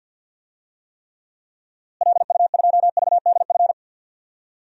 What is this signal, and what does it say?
Morse code, heard as a keyed tone, about 51 wpm, 700 Hz. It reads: LU2VDF
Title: DXCC-100 Call Signs - 6 Characters - Repeated 3x - 50wpm